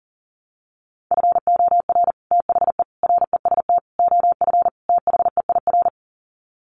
Morse code, heard as a keyed tone, about 40 wpm, 700 Hz. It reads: FOR THE REST OF THEIR
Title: Sets of 5 Words Encore2 40wpm